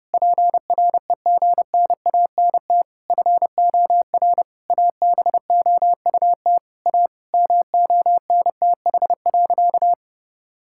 Morse code, heard as a keyed tone, about 30 wpm, 700 Hz. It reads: PREGNANT FOR ABOUT A MONTH.
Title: Sentences from Top 1000 Words - Review Part 6 - 30wpm